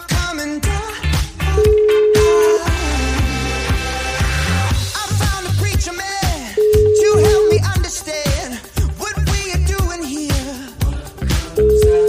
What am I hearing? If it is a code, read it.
Ringback tone (Europe, 425 Hz), with noise.